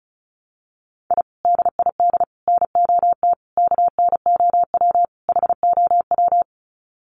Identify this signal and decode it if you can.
Morse code, heard as a keyed tone, about 34 wpm, 700 Hz. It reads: I DID NOT KNOW HOW
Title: Sets of 5 Words Encore 35wpm